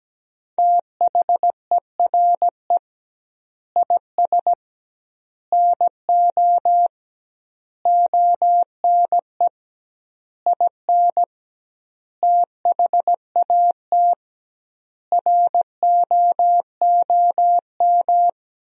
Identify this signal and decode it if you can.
Morse code, heard as a keyed tone, about 17 wpm, 700 Hz. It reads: THERE IS NO ONE IN THAT ROOM
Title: 2x Word-Spacing - Sentences from Top 500 Words - 17wpm